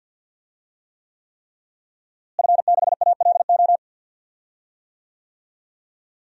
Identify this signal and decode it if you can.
Morse code, heard as a keyed tone, about 47 wpm, 700 Hz. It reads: F6ALO
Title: DXCC-100 Call Signs - 5 Characters - Repeated 3x - 50wpm